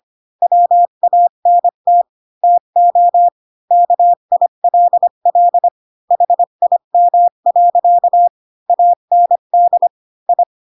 Morse code, heard as a keyed tone, about 25 wpm, 700 Hz. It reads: WANT TO KILL HIM. AND I
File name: Pinocchio / Pinocchio ch7 20 25 ik1hge